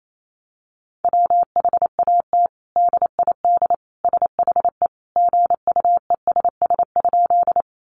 Morse code, heard as a keyed tone, about 28 wpm, 700 Hz. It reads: WHAT DID SHE GUESS?
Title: Sentences from Top 1500 Words 28wpm